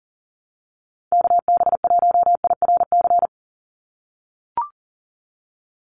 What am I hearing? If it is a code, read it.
Morse code, heard as a keyed tone, about 40 wpm, 700 Hz. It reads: KB1IRC
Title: USA Call Signs 40wpm